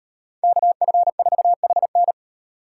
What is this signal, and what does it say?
Morse code, heard as a keyed tone, about 38 wpm, 700 Hz. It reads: KF4HN